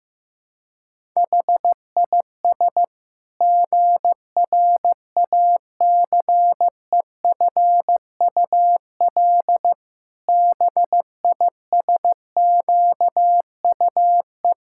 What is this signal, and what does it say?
Morse code, heard as a keyed tone, about 15 wpm, 700 Hz. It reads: HIS GRACEFUL BISQUE
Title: Pangram Challenge v2 - 15wpm